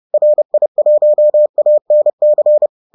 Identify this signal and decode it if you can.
Morse code, heard as a keyed tone, about 30 wpm, 600 Hz. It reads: RI1ANC